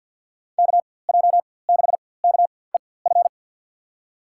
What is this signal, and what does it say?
Morse code, heard as a keyed tone, about 48 wpm, spaced as for 23 wpm, 700 Hz. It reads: KJ6KEF
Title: USA Call Signs - 6 Characters - 4x Character Spacing - 50wpm